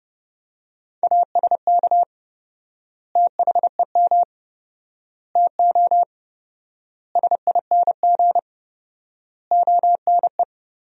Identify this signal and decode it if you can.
Morse code, heard as a keyed tone, about 32 wpm, 700 Hz. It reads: ASK THEM TO SING ONE
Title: 4x Word-Spacing - Sentences from Top 400 Words - 30wpm